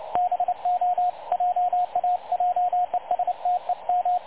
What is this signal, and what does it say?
Morse code, heard as a keyed tone, about 29 wpm, 720 Hz, with radio noise. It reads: BO JA JESTEM
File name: dlazaawansowanych / nagraniazpasma / websdr recording 3565.0kHz